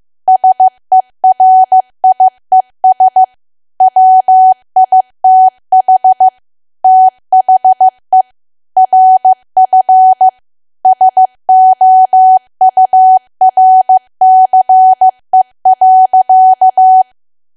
Morse code, heard as a keyed tone, about 15 wpm, 750 Hz. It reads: SERIES WITH THE RF SOURCE.
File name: cwa / W1AW-Level2 / 15WPM